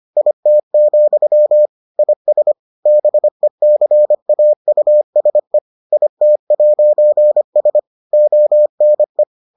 Morse code, heard as a keyed tone, about 25 wpm, 600 Hz. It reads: IT, IS BECAUSE IT'S ONE